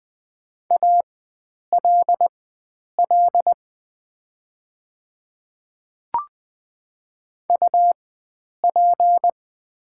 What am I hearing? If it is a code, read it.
Morse code, heard as a keyed tone, about 20 wpm, spaced as for 9 wpm, 700 Hz. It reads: ALL UP
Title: 4x Character Spacing - Top 100 Words - 20wpm